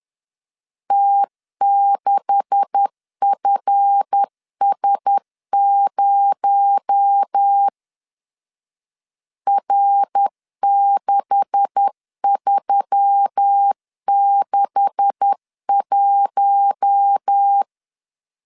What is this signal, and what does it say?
Morse code, heard as a keyed tone, about 10 wpm, 800 Hz. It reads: T6FS0 R6361
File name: LEZIONI / 9 286 10wpm 2735